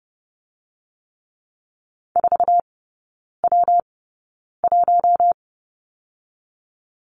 Morse code, heard as a keyed tone, about 30 wpm, spaced as for 9 wpm, 700 Hz. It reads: 4W1
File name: Lesson 19e - Letter w - Three Characters 30wpm